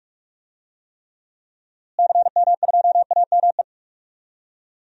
Morse code, heard as a keyed tone, about 42 wpm, 700 Hz. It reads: CM2AME
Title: DXCC-100 Call Signs - 6 Characters - Repeated 2x - 45wpm